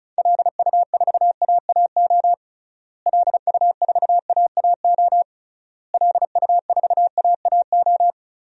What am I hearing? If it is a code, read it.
Morse code, heard as a keyed tone, about 34 wpm, 700 Hz. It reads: LU4AAO LU4AAO LU4AAO
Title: DXCC-100 Call Signs - 6 Characters - Repeated 3x - 35wpm